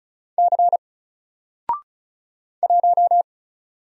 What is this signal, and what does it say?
Morse code, heard as a keyed tone, about 36 wpm, 700 Hz. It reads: C 1
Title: Single Character (Letter:Number:Punctuation:Prosign) 35wpm